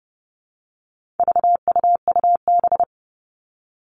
Morse code, heard as a keyed tone, about 30 wpm, 700 Hz. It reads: VUUB
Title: All Cut Numbers - 4-Digit 30wpm